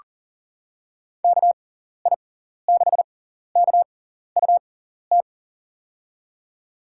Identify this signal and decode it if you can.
Morse code, heard as a keyed tone, about 40 wpm, spaced as for 14 wpm, 700 Hz. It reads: KI6KUT
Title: USA Call Signs - 6x Character Spacing - 40wpm